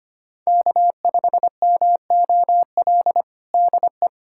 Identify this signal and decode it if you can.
Morse code, heard as a keyed tone, about 25 wpm, 700 Hz. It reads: K5MOL DE
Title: USA Callsign DE Callsign - 5 Characters - 25wpm